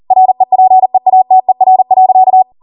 Morse code, heard as a keyed tone, about 40 wpm, 750 Hz. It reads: REPEATER.
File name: Morse / 40WPM